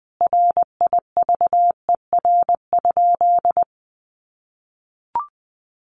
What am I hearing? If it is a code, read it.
Morse code, heard as a keyed tone, about 19 wpm, 700 Hz. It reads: RIVER?
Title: Sentences from Top 1400 Words 20wpm